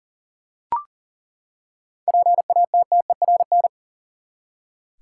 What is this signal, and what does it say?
Morse code, heard as a keyed tone, about 40 wpm, 700 Hz. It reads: PATTERN
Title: Seven Letter Words - 40wpm